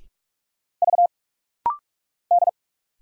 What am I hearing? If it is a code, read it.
Morse code, heard as a keyed tone, about 44 wpm, 700 Hz. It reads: V D
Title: Single Letter-Number Alt - Rapid-Fire 45wpm